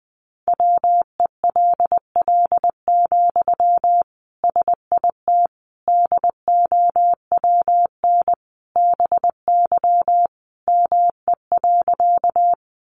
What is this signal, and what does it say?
Morse code, heard as a keyed tone, about 20 wpm, 700 Hz. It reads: WELL, SIT DOWN BY ME.
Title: Sentences from Top 500 Words 20wpm